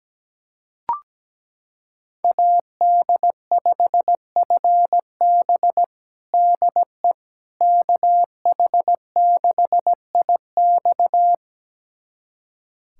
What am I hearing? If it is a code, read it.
Morse code, heard as a keyed tone, about 17 wpm, 700 Hz. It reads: AD5FB DE KH6IX
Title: USA Callsign DE Callsign - 5 Characters - 17wpm